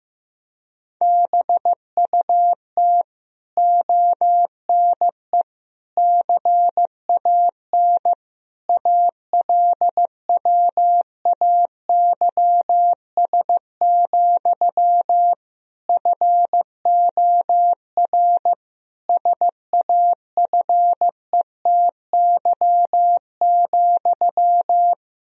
Morse code, heard as a keyed tone, about 15 wpm, 700 Hz. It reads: BUT ONE CAN ALWAYS, FOR SAFETY,
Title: Sentences from Top 1000 Words - Review Part 4 - 15wpm